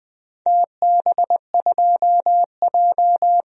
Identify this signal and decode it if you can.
Morse code, heard as a keyed tone, about 21 wpm, 700 Hz. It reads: TB2J
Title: DXCC-100 Call Signs - 4 Characters - 20wpm